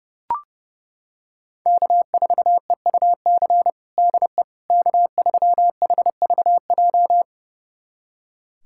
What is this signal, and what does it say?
Morse code, heard as a keyed tone, about 32 wpm, 700 Hz. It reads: K4EUC DE K3HVJ
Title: USA Callsign DE Callsign - 5 Characters - 30wpm